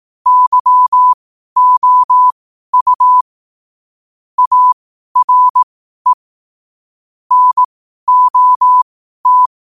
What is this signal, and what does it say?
Morse code, heard as a keyed tone, about 18 wpm, spaced as for 12 wpm, 1000 Hz. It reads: YOU ARE NOT
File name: qotdfiles / 12wpm